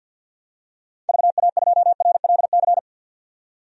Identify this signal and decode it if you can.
Morse code, heard as a keyed tone, about 52 wpm, 700 Hz. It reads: VA2RLC
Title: DXCC-100 Call Signs - 6 Characters - 50wpm